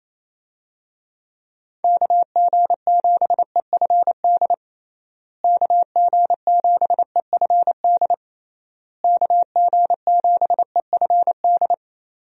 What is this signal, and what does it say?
Morse code, heard as a keyed tone, about 27 wpm, 700 Hz. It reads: KG7EFD KG7EFD KG7EFD
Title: USA Call Signs - 6 Characters - Repeated 3x - 28wpm